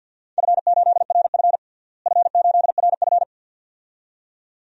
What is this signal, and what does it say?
Morse code, heard as a keyed tone, about 50 wpm, 700 Hz. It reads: F8RF F8RF
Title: DXCC-100 Call Signs - 4 Characters - Repeated 2x - 50wpm